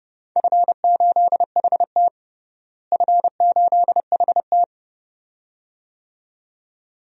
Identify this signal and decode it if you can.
Morse code, heard as a keyed tone, about 30 wpm, 700 Hz. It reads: F8HT F8HT
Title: DXCC-100 Call Signs - 4 Characters - Repeated 2x - 30wpm